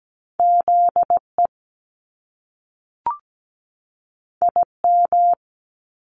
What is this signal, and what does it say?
Morse code, heard as a keyed tone, about 17 wpm, 700 Hz. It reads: ZE IM
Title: Top Two Letter Combinations in Words 17wpm